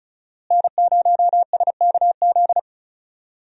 Morse code, heard as a keyed tone, about 34 wpm, 700 Hz. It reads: N0SKZ
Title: USA Call Signs - 5 Characters - Repeated 3x - 35wpm